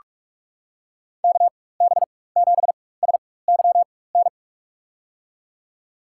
Morse code, heard as a keyed tone, about 46 wpm, spaced as for 21 wpm, 700 Hz. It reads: KB7SYN
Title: USA Call Signs - 4x Character Spacing - 45wpm